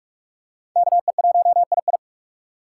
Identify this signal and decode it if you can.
Morse code, heard as a keyed tone, about 47 wpm, 700 Hz. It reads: KE1II
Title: USA Call Signs - 5 Characters - 3x Character Spacing - 45wpm